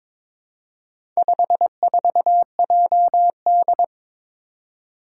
Morse code, heard as a keyed tone, about 22 wpm, 700 Hz. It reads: 54JD
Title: Four Challenging Letter-Number Combinations 22wpm